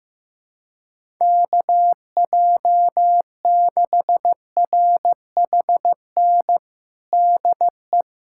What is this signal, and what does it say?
Morse code, heard as a keyed tone, about 15 wpm, 700 Hz. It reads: KJ6RHN DE
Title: USA Callsign DE Callsign 15wpm